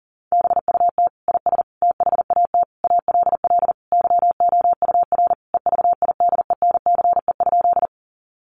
Morse code, heard as a keyed tone, about 40 wpm, 700 Hz. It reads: BUT IS THAT ALL YOUR EVIDENCE?
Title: Sentences from Top 1300 Words 40wpm